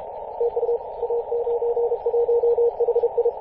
Morse code, heard as a keyed tone, about 34 wpm, 470 Hz, with radio noise. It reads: TU A91HI
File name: posts / 2016-may-01-0643z-a91hi-21-040mhz-cw